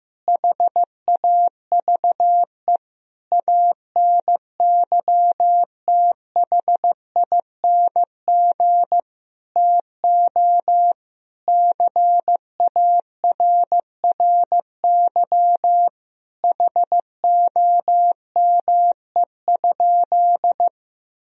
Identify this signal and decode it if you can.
Morse code, heard as a keyed tone, about 15 wpm, 700 Hz. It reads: HAVE ANYTHING TO CARRY HOME?
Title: Sentences from Top 1000 Words - Review Part 1 - 15wpm